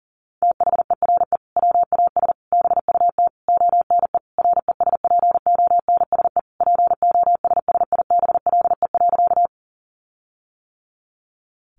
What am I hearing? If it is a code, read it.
Morse code, heard as a keyed tone, about 40 wpm, 700 Hz. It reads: THERE WAS BUT ONE RESPONSE POSSIBLE.
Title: Sentences from Top 1400 Words 40wpm